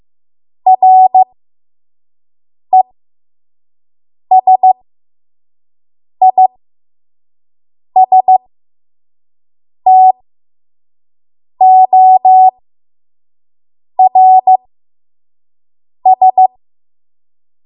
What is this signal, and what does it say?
Morse code, heard as a keyed tone, about 15 wpm, spaced as for 5 wpm, 750 Hz. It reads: RESISTORS
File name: Morse / 05WPM